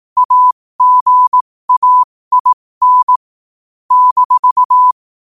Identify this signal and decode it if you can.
Morse code, heard as a keyed tone, about 18 wpm, spaced as for 15 wpm, 1000 Hz. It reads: AGAIN -
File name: qotdfiles / 15wpm